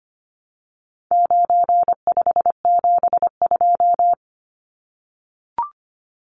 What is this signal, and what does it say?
Morse code, heard as a keyed tone, about 25 wpm, 700 Hz. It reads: 9572
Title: Numbers - 4-Digit 25wpm